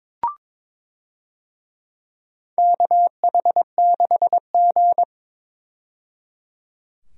Morse code, heard as a keyed tone, about 21 wpm, 700 Hz. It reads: KH6G
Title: DXCC-100 Call Signs - 4 Characters - 22wpm